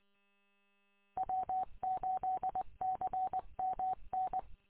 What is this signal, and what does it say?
Morse code, heard as a keyed tone, about 23 wpm, 750 Hz. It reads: W8CMN
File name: w8cmn- -combined